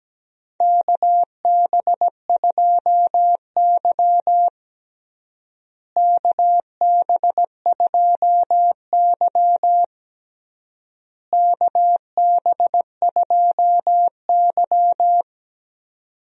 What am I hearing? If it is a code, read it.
Morse code, heard as a keyed tone, about 17 wpm, 700 Hz. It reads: KB2Y KB2Y KB2Y
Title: USA Call Signs - 4 Characters - Repeated 3x - 17wpm